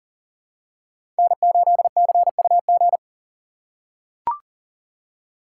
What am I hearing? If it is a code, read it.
Morse code, heard as a keyed tone, about 40 wpm, 700 Hz. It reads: N8CUG
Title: USA Call Signs - 2x Character Spacing - 40wpm